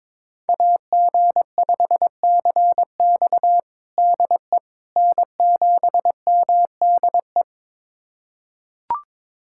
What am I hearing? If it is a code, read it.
Morse code, heard as a keyed tone, about 22 wpm, 700 Hz. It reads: AG5CX DE N7MDE
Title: USA Callsign DE Callsign - 5 Characters - 22wpm